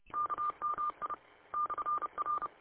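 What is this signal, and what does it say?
Morse code, heard as a keyed tone, about 30 wpm, 1200 Hz. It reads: KMI /R
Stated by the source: Morse code from an audio archive of station w8um.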